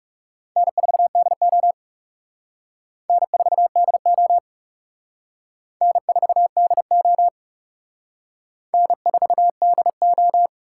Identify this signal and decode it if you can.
Morse code, heard as a keyed tone, about 35 wpm, 700 Hz. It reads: N4DO N4DO N4DO N4DO
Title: Speed Racing - USA 1x2 Call Signs 30wpm